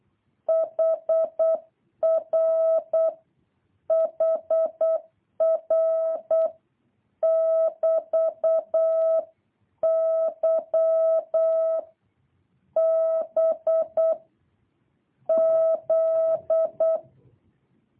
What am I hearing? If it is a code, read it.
Morse code, heard as a keyed tone, about 8 wpm, 660 Hz. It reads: HR HR <BT> Y B Z
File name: jh8cbh / 2023cw